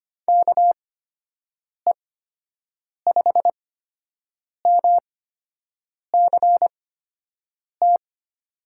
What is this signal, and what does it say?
Morse code, heard as a keyed tone, about 25 wpm, spaced as for 7 wpm, 700 Hz. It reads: KE5MCT
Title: USA Call Signs - 6 Characters - 8x Character Spacing - 25wpm